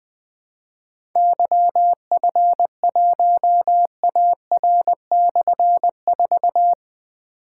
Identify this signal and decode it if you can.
Morse code, heard as a keyed tone, about 20 wpm, 700 Hz. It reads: YF1AR/4
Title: Difficult Call Signs 20wpm